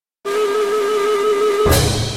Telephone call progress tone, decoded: congestion tone